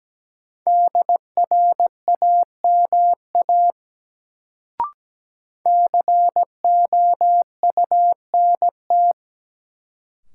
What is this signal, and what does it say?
Morse code, heard as a keyed tone, about 17 wpm, 700 Hz. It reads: DRAMA COUNT
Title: Sentences from Top 1500 Words - New Word Drill Alt - 17wpm